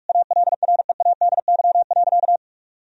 Morse code, heard as a keyed tone, about 46 wpm, 700 Hz. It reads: ALREADY.
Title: Sentences from Top 1000 Words - Review Part 2 - 45wpm